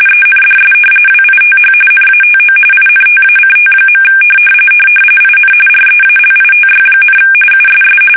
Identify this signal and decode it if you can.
RTTY, about 45.45 baud. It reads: ENDS WITH NATO INTACT. IN REALITY, USS
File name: fsk850 / battleshipnj